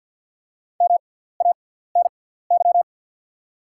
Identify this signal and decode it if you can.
Morse code, heard as a keyed tone, about 53 wpm, spaced as for 17 wpm, 690 Hz. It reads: MANY
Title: 6x Character Spacing - Top 500 Words - 50wpm